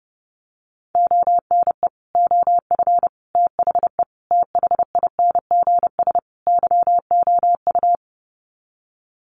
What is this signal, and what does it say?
Morse code, heard as a keyed tone, about 29 wpm, 700 Hz. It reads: ONE OF THE THINGS YOU
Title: Sets of 5 Words Encore2 30wpm